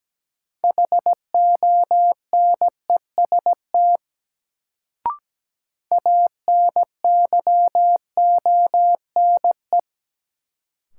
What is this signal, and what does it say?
Morse code, heard as a keyed tone, about 17 wpm, 700 Hz. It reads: HONEST ANYONE
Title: Sets of 5 Words Encore - New Word Drill - 17wpm